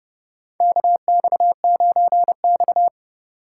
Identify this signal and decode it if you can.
Morse code, heard as a keyed tone, about 30 wpm, 700 Hz. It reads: KX9X
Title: USA Call Signs - 6x Character Spacing - 30wpm